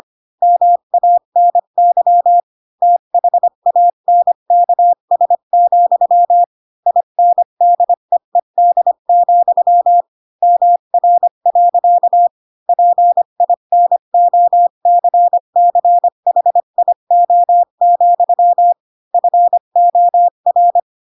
Morse code, heard as a keyed tone, about 25 wpm, 700 Hz. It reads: MANY THANKS, INDEED, MR. PINOCCHIO, FOR
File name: Pinocchio / Pinocchio ch5 20 25 ik1hge